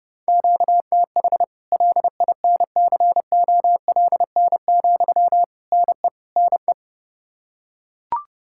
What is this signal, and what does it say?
Morse code, heard as a keyed tone, about 30 wpm, 700 Hz. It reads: QTH LINCOLN, NE NE
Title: Practice QSO - Exchange 1 - Beginning - 30wpm